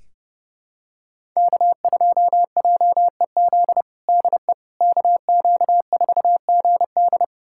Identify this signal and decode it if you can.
Morse code, heard as a keyed tone, about 32 wpm, 700 Hz. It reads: K2JEZ DE KQ4GD
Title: USA Callsign DE Callsign - 5 Characters - 30wpm